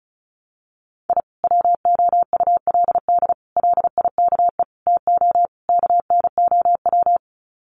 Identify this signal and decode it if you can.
Morse code, heard as a keyed tone, about 34 wpm, 700 Hz. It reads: I WOULD LIKE TO KNOW
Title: Sets of 5 Words Encore 35wpm